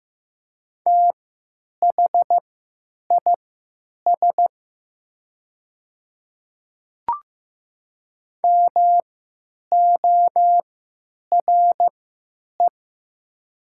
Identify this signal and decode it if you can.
Morse code, heard as a keyed tone, about 15 wpm, spaced as for 9 wpm, 700 Hz. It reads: THIS MORE
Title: 3x Character Spacing - Top 100 Words - 15wpm